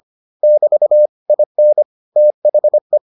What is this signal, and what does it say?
Morse code, heard as a keyed tone, about 25 wpm, 600 Hz. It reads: <BT> IN THE